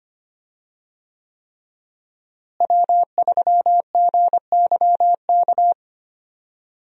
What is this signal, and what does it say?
Morse code, heard as a keyed tone, about 25 wpm, 700 Hz. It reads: W3GYK
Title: USA Call Signs - 5 Characters - Repeated 3x - 25wpm